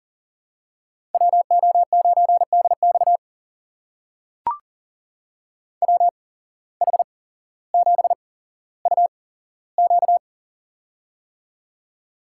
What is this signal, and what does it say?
Morse code, heard as a keyed tone, about 39 wpm, spaced as for 17 wpm, 700 Hz. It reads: WO9DX WH7UQ
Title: USA Call Signs - 5 Characters - 8x Character Spacing - 40wpm